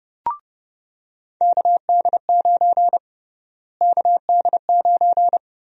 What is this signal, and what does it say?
Morse code, heard as a keyed tone, about 30 wpm, 700 Hz. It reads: KD9 KD9
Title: USA Call Sign Prefixes - Repeated 2x - 30wpm